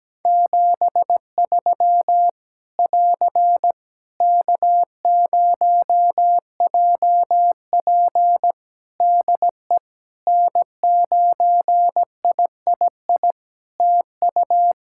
Morse code, heard as a keyed tone, about 17 wpm, 700 Hz. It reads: 73 <AR> K0JP DE N9III TU
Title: Practice QSO - Exchange 4 - Ending - 17wpm